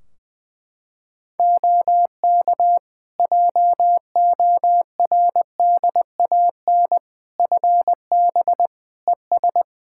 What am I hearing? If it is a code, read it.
Morse code, heard as a keyed tone, about 20 wpm, 700 Hz. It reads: OK JORDAN FB ES
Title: Practice QSO - Exchange 2 - Info - 20wpm